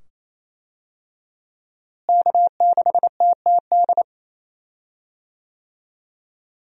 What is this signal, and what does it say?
Morse code, heard as a keyed tone, about 28 wpm, 700 Hz. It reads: K6TTD
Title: USA Call Signs - 5 Characters - Repeated 2x - 28wpm